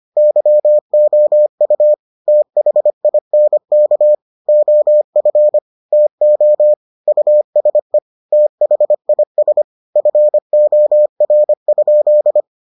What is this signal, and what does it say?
Morse code, heard as a keyed tone, about 25 wpm, 600 Hz. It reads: YOU THINK OF TO USE THIS FOR?